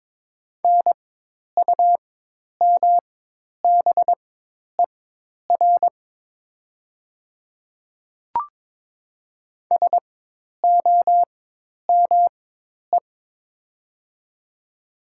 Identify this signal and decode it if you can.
Morse code, heard as a keyed tone, about 22 wpm, spaced as for 10 wpm, 700 Hz. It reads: NUMBER SOME
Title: 4x Character Spacing - Top 100 Words - 22wpm